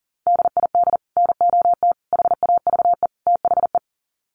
Morse code, heard as a keyed tone, about 40 wpm, 700 Hz. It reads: DID NOT HAVE THE
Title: Sets of 4 Words 40wpm